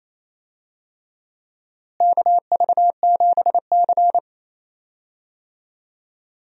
Morse code, heard as a keyed tone, about 28 wpm, 700 Hz. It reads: KV7C
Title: USA Call Signs - 4 Characters - Repeated 3x - 28wpm